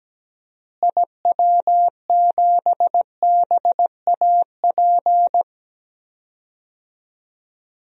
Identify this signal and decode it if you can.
Morse code, heard as a keyed tone, about 17 wpm, 700 Hz. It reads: IW7BAP
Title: DXCC-100 Call Signs - 6 Characters - Repeated 3x - 17wpm